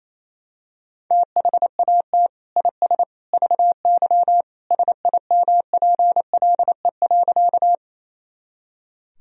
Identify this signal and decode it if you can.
Morse code, heard as a keyed tone, about 28 wpm, 700 Hz. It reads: THAT IS VY SIMPLE.
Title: Sentences from Top 400 Words and Abbreviations 28wpm